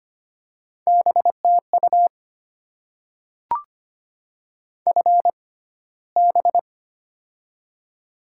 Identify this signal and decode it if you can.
Morse code, heard as a keyed tone, about 25 wpm, spaced as for 15 wpm, 690 Hz. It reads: BTU FB
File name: Top 20 QSO Elements - 6x Character Spacing - 25wpm